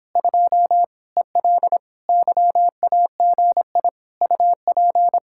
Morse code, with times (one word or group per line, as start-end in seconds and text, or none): 0.2-0.8 2
1.2-1.8 EL
2.1-3.9 YAGI
4.2-5.2 UP